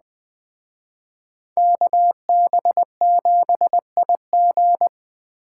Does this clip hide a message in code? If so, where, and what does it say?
Yes: Morse code from 1.6 to 4.9 s, heard as a keyed tone, about 20 wpm, 700 Hz. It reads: KB7IG